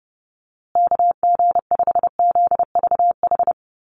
Morse code, heard as a keyed tone, about 30 wpm, 700 Hz. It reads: KG5ZVH
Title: USA Call Signs Repeated 2x 30wpm